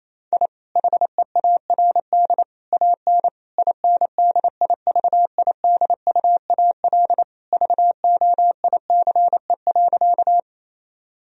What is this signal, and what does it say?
Morse code, heard as a keyed tone, about 28 wpm, 700 Hz. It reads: I HEARD AN INDIVIDUAL VOICE.
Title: Sentences from Top 1000 Words - Review Part 4 - 28wpm